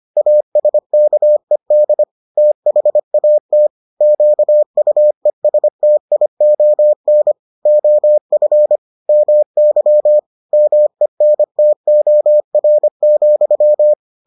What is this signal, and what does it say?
Morse code, heard as a keyed tone, about 25 wpm, 600 Hz. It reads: ASKED THAT QUESTION OF MY MENTOR,